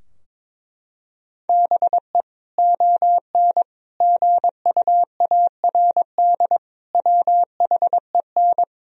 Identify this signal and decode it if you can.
Morse code, heard as a keyed tone, about 21 wpm, 700 Hz. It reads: BE ON GUARD WHEN
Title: Sentences from Top 1000 Words - Review Part 4 - 22wpm